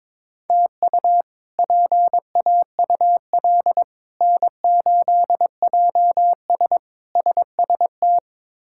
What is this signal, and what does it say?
Morse code, heard as a keyed tone, about 22 wpm, 700 Hz. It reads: TU PAUL N8JS SST
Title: K1USN SST QSO Practice - 22wpm